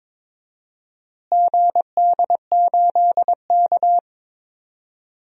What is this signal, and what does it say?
Morse code, heard as a keyed tone, about 21 wpm, 700 Hz. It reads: GD8K
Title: DXCC-100 Call Signs - 4 Characters - Repeated 2x - 22wpm